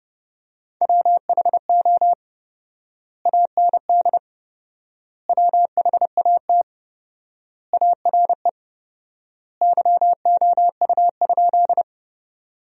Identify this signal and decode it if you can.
Morse code, heard as a keyed tone, about 32 wpm, 700 Hz. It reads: WHO AND WHAT ARE YOU?
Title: 4x Word-Spacing - Sentences from Top 100 Words - 30wpm